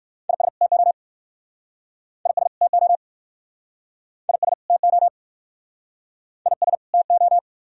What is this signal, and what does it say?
Morse code, heard as a keyed tone, about 60 wpm, 700 Hz. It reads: IS TO IS TO IS TO IS TO